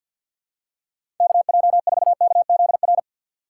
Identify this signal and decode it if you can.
Morse code, heard as a keyed tone, about 53 wpm, 700 Hz. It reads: KJ4KZR
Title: USA Call Signs - 6 Characters - 3x Character Spacing - 50wpm